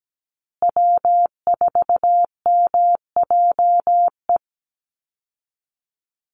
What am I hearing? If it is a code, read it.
Morse code, heard as a keyed tone, about 17 wpm, 700 Hz. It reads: W4MJE
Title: Call Signs Repeated 3x - 17wpm